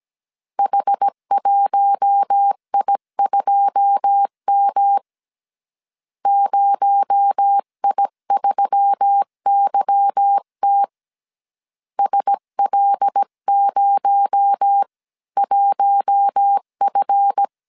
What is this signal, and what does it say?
Morse code, heard as a keyed tone, about 17 wpm, 800 Hz. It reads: H1I2M 0I3YT SL01F
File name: 1500C S B 16wpm